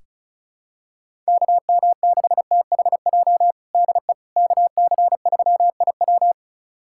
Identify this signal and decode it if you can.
Morse code, heard as a keyed tone, about 34 wpm, 700 Hz. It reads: KM6THJ DE KC3IW
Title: USA Callsign DE Callsign 35wpm